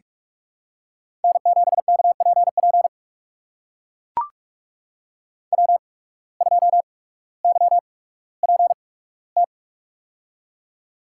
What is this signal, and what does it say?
Morse code, heard as a keyed tone, about 44 wpm, spaced as for 19 wpm, 700 Hz. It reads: N7KPP W2YPT
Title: USA Call Signs - 5 Characters - 8x Character Spacing - 45wpm